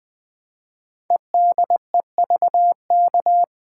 Morse code, heard as a keyed tone, about 20 wpm, 700 Hz. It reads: EDEVK
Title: Code Groups - 5 Letters - 20wpm